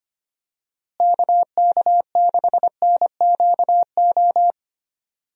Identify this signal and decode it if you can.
Morse code, heard as a keyed tone, about 25 wpm, 700 Hz. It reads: KK6NQO
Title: USA Call Signs - 6 Characters - Repeated 2x - 25wpm